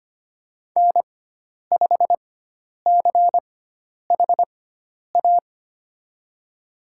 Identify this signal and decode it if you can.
Morse code, heard as a keyed tone, about 25 wpm, spaced as for 10 wpm, 700 Hz. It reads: N5CHA
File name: Lesson 16g - Letter c - Call Signs 25wpm